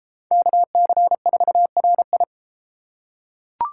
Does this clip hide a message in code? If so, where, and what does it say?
Yes: Morse code from 0.3 to 2.2 s, heard as a keyed tone, about 31 wpm, 700 Hz. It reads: KC4RI